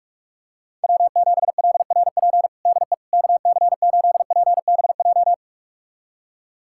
Morse code, heard as a keyed tone, about 46 wpm, 700 Hz. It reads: W7LRP DE KC8PBJ
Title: USA Callsign DE Callsign 45wpm